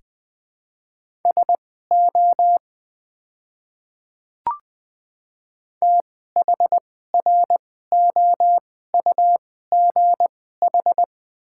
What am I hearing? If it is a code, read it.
Morse code, heard as a keyed tone, about 20 wpm, spaced as for 14 wpm, 700 Hz. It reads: SO THROUGH
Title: 2x Character Spacing - Top 200 Words - 20wpm